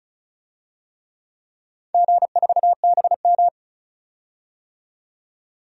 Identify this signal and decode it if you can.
Morse code, heard as a keyed tone, about 34 wpm, 700 Hz. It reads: G4BM
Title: DXCC-100 Call Signs - 4 Characters - 35wpm